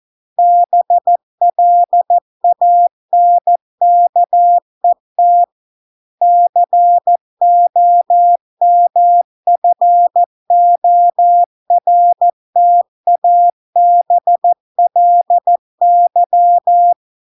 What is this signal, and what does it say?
Morse code, heard as a keyed tone, about 14 wpm, 700 Hz. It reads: BLANKET COMFORTABLY